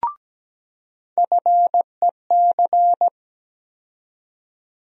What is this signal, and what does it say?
Morse code, heard as a keyed tone, about 17 wpm, 700 Hz. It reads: FEC
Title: USA Call Sign Postfixes - 3 Letters 17wpm